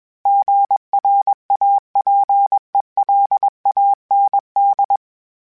Morse code, heard as a keyed tone, about 21 wpm, 800 Hz. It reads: GRAPELAND
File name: fromSMs / FromSMs mp3s / Cities@20